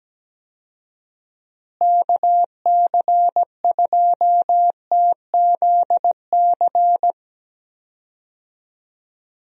Morse code, heard as a keyed tone, about 17 wpm, 700 Hz. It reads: KC2TZC
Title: USA Call Signs - 6 Characters - Repeated 2x - 17wpm